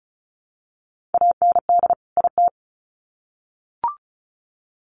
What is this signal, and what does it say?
Morse code, heard as a keyed tone, about 34 wpm, 700 Hz. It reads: AND IT
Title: Sets of 2 Words 35wpm